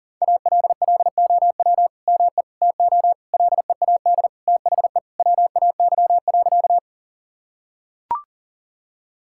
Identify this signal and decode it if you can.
Morse code, heard as a keyed tone, about 37 wpm, 700 Hz. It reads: ALLOW ME TO LEAD THE WAY.
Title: Sentences from Top 1000 Words - Review Part 5 - 40wpm